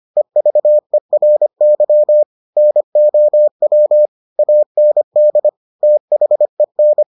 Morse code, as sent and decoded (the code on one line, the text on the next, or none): . ...- . .-. -.-- / -. --- .-- / .- -. -.. / - .... -.
EVERY NOW AND THEN